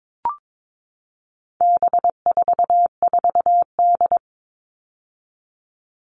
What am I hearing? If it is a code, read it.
Morse code, heard as a keyed tone, about 22 wpm, 700 Hz. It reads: B44D
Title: All Cut Numbers - 4-Digit 22wpm